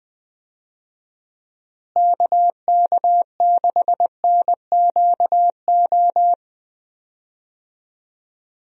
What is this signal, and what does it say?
Morse code, heard as a keyed tone, about 20 wpm, 700 Hz. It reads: KK6NQO
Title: USA Call Signs - 6 Characters - Repeated 2x - 20wpm